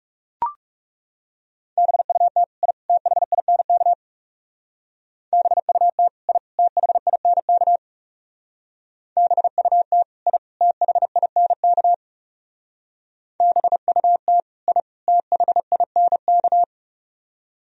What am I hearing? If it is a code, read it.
Morse code, heard as a keyed tone, about 35 wpm, 700 Hz. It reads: BUT I THINK BUT I THINK BUT I THINK BUT I THINK
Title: Speed-Racing - Sets of 3 Words 30wpm